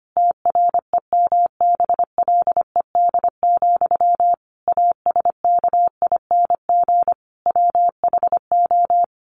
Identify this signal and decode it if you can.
Morse code, heard as a keyed tone, about 25 wpm, 700 Hz. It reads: TREMBLED, ASKING WHO
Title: A Sound Of Thunder 25wpm